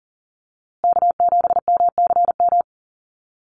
Morse code, heard as a keyed tone, about 40 wpm, 700 Hz. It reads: K7MCM
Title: USA Call Signs Repeated 3x 40wpm